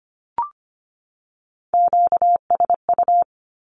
Morse code, heard as a keyed tone, about 25 wpm, 700 Hz. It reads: QSU
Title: Q-Codes 25wpm